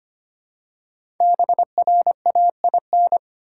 Morse code, heard as a keyed tone, about 25 wpm, 700 Hz. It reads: BRAIN
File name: 5 Letter Words - First 3 Letters Given - 25wpm